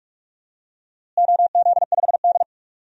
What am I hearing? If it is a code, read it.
Morse code, heard as a keyed tone, about 46 wpm, 700 Hz. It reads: OZ5D
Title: DXCC-100 Call Signs - 4 Characters - Repeated 2x - 45wpm